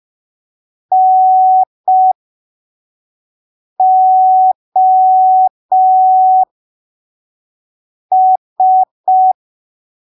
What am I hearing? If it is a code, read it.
Morse code, heard as a keyed tone, about 5 wpm, spaced as for 3 wpm, 750 Hz. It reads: NOS
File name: arquivos / texto / texto 02 5ppm